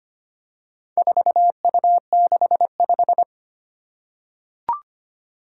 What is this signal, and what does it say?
Morse code, heard as a keyed tone, about 25 wpm, 700 Hz. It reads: 4U65
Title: Four Challenging Letter-Number Combinations 25wpm